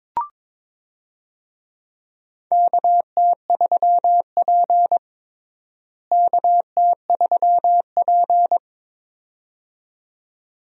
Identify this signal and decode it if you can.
Morse code, heard as a keyed tone, about 22 wpm, 700 Hz. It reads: KT3P KT3P
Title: USA Call Signs - 4 Characters - Repeated 2x - 22wpm